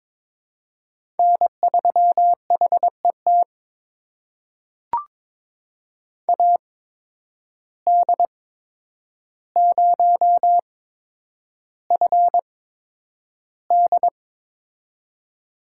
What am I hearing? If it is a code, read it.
Morse code, heard as a keyed tone, about 22 wpm, spaced as for 9 wpm, 700 Hz. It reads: N3HET AD0FD
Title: USA Call Signs - 8x Character Spacing - 22wpm